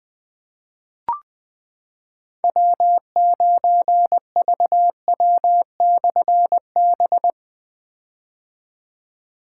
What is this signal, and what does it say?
Morse code, heard as a keyed tone, about 21 wpm, 700 Hz. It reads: W9VW/B
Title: Difficult Call Signs 20wpm